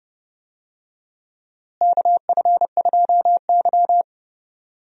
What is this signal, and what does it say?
Morse code, heard as a keyed tone, about 32 wpm, 700 Hz. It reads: KF2Y
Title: USA Call Signs - 4 Characters - Repeated 2x - 30wpm